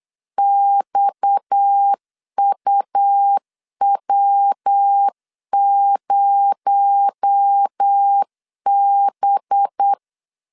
Morse code, heard as a keyed tone, about 8 wpm, 800 Hz. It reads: XUW0B
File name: LEZIONI / 6 PKX0 / 6 PKX0 8wpm 3341